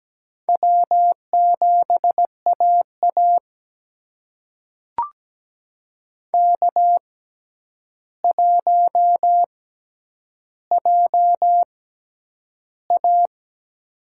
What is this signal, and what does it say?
Morse code, heard as a keyed tone, about 17 wpm, spaced as for 9 wpm, 700 Hz. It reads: W7AA K1JA
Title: USA Call Signs - 4 Characters - 6x Character Spacing - 17wpm